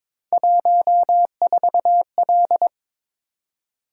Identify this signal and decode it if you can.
Morse code, heard as a keyed tone, about 22 wpm, 700 Hz. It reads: 14L
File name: practice / 22wpm / Three Letter-Number Combinations - 22wpm